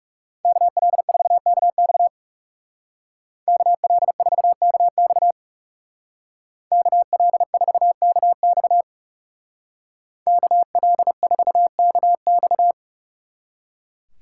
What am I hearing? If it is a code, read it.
Morse code, heard as a keyed tone, about 40 wpm, 700 Hz. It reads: KL4KX KL4KX KL4KX KL4KX